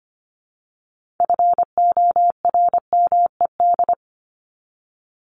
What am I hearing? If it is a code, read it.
Morse code, heard as a keyed tone, about 25 wpm, 700 Hz. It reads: FORMED
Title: ED Suffix - Top 100 Words with Base Words 25wpm